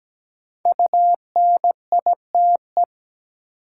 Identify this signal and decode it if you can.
Morse code, heard as a keyed tone, about 17 wpm, 700 Hz. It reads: UNITE
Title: Sentences from Top 1000 Words - Review Part 5 - New Word Drill - 17wpm